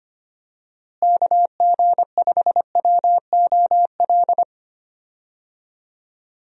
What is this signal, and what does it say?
Morse code, heard as a keyed tone, about 25 wpm, 700 Hz. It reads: KG5WOL
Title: USA Call Signs - 6 Characters - Repeated 3x - 25wpm